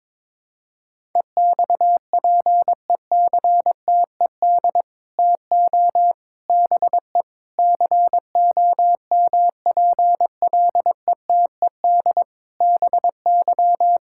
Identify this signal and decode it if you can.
Morse code, heard as a keyed tone, about 22 wpm, 700 Hz. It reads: EXPECTED TO BE COMPLETED BY
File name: practice / 22wpm / Sets of 8 Words - 22wpm